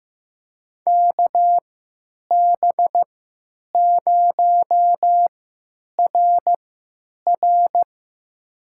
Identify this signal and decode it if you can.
Morse code, heard as a keyed tone, about 15 wpm, spaced as for 9 wpm, 700 Hz. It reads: KB0RR
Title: USA Call Signs - 5 Characters - 3x Character Spacing - 15wpm